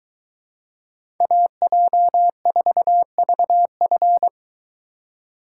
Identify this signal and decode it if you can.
Morse code, heard as a keyed tone, about 23 wpm, 700 Hz. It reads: AJ4VF